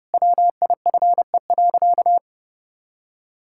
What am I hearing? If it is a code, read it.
Morse code, heard as a keyed tone, about 32 wpm, 700 Hz. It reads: WIFE.
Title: Pangram Challenge v2 - 30wpm